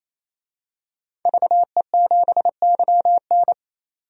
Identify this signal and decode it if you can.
Morse code, heard as a keyed tone, about 28 wpm, 700 Hz. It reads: VE7YN
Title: DXCC-100 Call Signs - 5 Characters - 28wpm